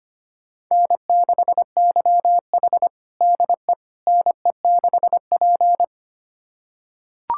Morse code, heard as a keyed tone, about 25 wpm, 700 Hz. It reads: N6YH DE NE6P
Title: USA Callsign DE Callsign - 4 Characters - 25wpm